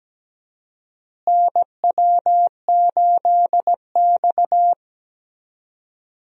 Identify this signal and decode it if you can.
Morse code, heard as a keyed tone, about 17 wpm, 700 Hz. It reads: NW8X
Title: USA Call Signs - 4 Characters - Repeated 3x - 17wpm